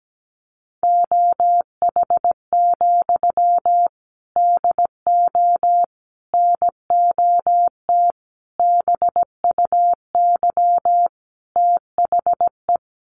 Morse code, heard as a keyed tone, about 17 wpm, 700 Hz. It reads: OH, DO NOT BUY THE
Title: Sentences from Top 1300 Words 17wpm